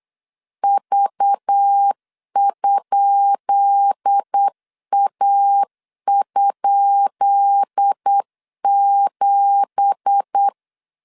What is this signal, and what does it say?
Morse code, heard as a keyed tone, about 8 wpm, 800 Hz. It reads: V?A?7
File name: LEZIONI / 10 SIGNS / 10B INT 8wpm 3534